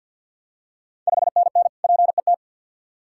Morse code, heard as a keyed tone, about 51 wpm, 690 Hz. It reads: 5NN PET
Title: Ontario QSO Party Exchange 50wpm